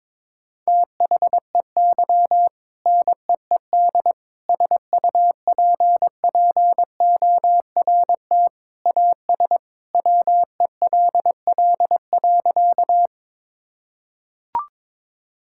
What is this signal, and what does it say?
Morse code, heard as a keyed tone, about 22 wpm, 700 Hz. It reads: THEY NEED SUPPORT AS WELL.
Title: Sentences from Top 1000 Words - Review Part 5 - 22wpm